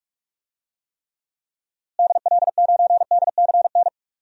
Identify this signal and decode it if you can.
Morse code, heard as a keyed tone, about 44 wpm, 700 Hz. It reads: DL9DCN
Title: DXCC-100 Call Signs - 6 Characters - Repeated 3x - 45wpm